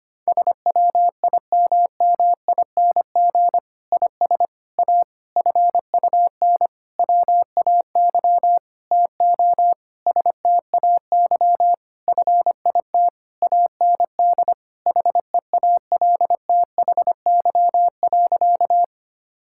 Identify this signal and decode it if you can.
Morse code, heard as a keyed tone, about 25 wpm, 700 Hz. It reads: SWIMMING IS A FUN WAY TO STAY FIT AND HEALTHY.
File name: practice / 25wpm / Difficult Sentences with Hints - 25wpm